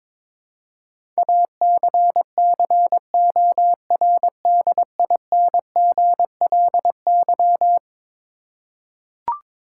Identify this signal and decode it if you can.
Morse code, heard as a keyed tone, about 22 wpm, 700 Hz. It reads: ACCORDINGLY
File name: practice / 22wpm / Eleven Letter Words - 22wpm